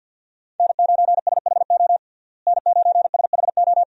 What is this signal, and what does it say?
Morse code, heard as a keyed tone, about 50 wpm, 700 Hz. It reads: N9SHO N9SHO
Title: USA Call Signs - 5 Characters - Repeated 2x - 50wpm